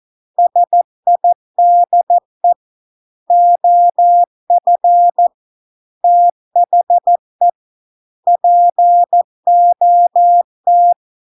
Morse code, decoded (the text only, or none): SIDE OF THE POT